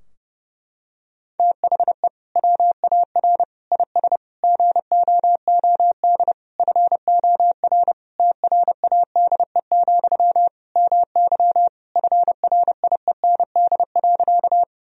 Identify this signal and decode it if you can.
Morse code, heard as a keyed tone, about 30 wpm, 700 Hz. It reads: THE WAR IS GOOD FOR TRADE, MY FRIEND.
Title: Sentences from Top 1000 Words - Review Part 4 - 30wpm